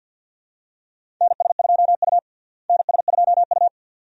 Morse code, heard as a keyed tone, about 50 wpm, 700 Hz. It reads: NS2U NS2U
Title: USA Call Signs - 4 Characters - Repeated 2x - 50wpm